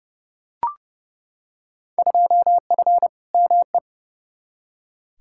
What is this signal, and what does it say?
Morse code, heard as a keyed tone, about 31 wpm, 700 Hz. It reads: 2F ME
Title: Field Day Exchanges - 30wpm